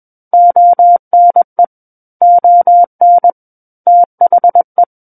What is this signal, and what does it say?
Morse code, heard as a keyed tone, about 21 wpm, 700 Hz. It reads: ONE ON THE
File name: Book One / 12-Chapter 12 - 21 WPM